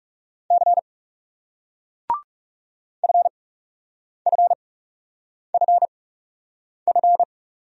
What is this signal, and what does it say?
Morse code, heard as a keyed tone, about 38 wpm, spaced as for 8 wpm, 700 Hz. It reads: C FFFF